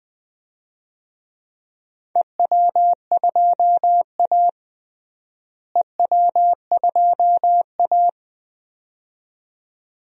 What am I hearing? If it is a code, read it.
Morse code, heard as a keyed tone, about 20 wpm, 700 Hz. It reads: EW2A EW2A
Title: DXCC-100 Call Signs - 4 Characters - Repeated 2x - 20wpm